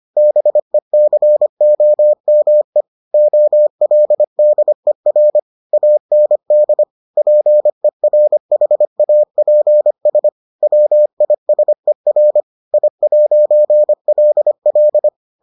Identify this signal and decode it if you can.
Morse code, heard as a keyed tone, about 25 wpm, 600 Hz. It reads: BECOME OLDER AND PERHAPS WISER I'LL